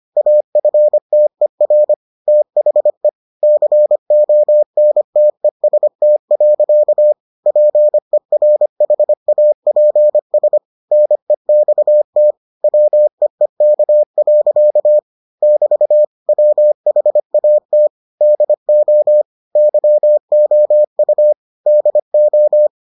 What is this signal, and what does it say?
Morse code, heard as a keyed tone, about 25 wpm, 600 Hz. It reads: AFTER THE CONTEST. PERHAPS NEXT WEEK. <BT> WHAT DO YOU DO